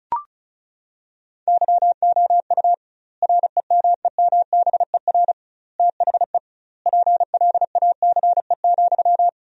Morse code, heard as a keyed tone, about 34 wpm, 700 Hz. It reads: YOU REMEMBER THE PLACE,
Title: 2x Word-Spacing - Sentences from Top 400 Words - 35wpm